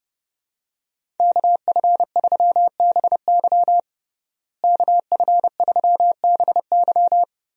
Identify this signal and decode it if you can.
Morse code, heard as a keyed tone, about 32 wpm, 700 Hz. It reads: KF3BY KF3BY
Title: USA Call Signs - 5 Characters - Repeated 3x - 30wpm